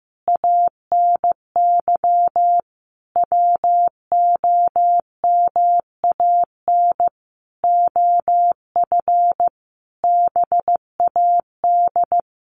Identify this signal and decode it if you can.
Morse code, heard as a keyed tone, about 15 wpm, 700 Hz. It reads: ANY WOMAN OF BAD
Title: Sentences from Top 1200 Words 15wpm